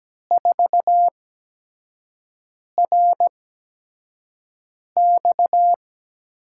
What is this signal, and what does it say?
Morse code, heard as a keyed tone, about 17 wpm, spaced as for 5 wpm, 700 Hz. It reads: 4RX
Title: USA Call Signs - 5 Characters - 8x Character Spacing - 17wpm